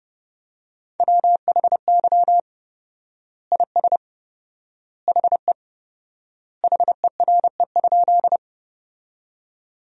Morse code, heard as a keyed tone, about 30 wpm, 700 Hz. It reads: WHY IS HE HERE?
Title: 4x Word-Spacing - Sentences from Top 200 Words - 30wpm